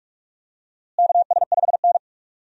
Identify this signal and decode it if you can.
Morse code, heard as a keyed tone, about 41 wpm, 700 Hz. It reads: KS5N
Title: USA Call Signs - 4 Characters - Repeated 3x - 45wpm